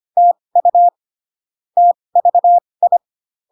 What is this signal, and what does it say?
Morse code, heard as a keyed tone, about 25 wpm, spaced as for 20 wpm, 700 Hz. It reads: TU TVI